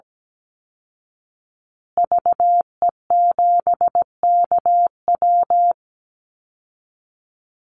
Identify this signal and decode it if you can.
Morse code, heard as a keyed tone, about 17 wpm, 700 Hz. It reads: VE7KW
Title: Call Signs Repeated 2x - 17wpm